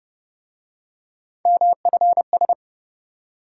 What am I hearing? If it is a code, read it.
Morse code, heard as a keyed tone, about 28 wpm, 700 Hz. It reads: MFS